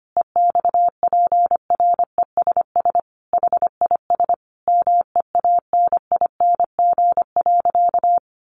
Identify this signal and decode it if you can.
Morse code, heard as a keyed tone, about 25 wpm, 700 Hz. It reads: EXPRESS HIS MEANING.
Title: Sentences from Top 1100 Words 25wpm